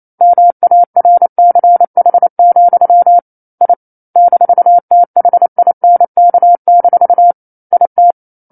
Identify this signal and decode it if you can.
Morse code, heard as a keyed tone, about 28 wpm, 700 Hz. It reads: MARCH, I -THINK- IT
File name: 09-Chapter-09 28 WPM